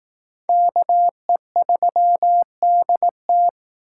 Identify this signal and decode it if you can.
Morse code, heard as a keyed tone, about 19 wpm, 700 Hz. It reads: KE3DT